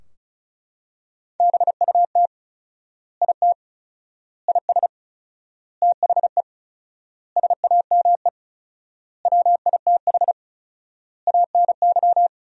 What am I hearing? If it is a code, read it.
Morse code, heard as a keyed tone, about 36 wpm, 700 Hz. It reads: BUT IT IS THE SAME WITH ANY
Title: 4x Word-Spacing - Sentences from Top 300 Words - 35wpm